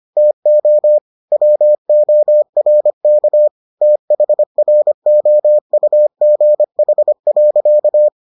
Morse code, heard as a keyed tone, about 25 wpm, 600 Hz. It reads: TO WORK THROUGH.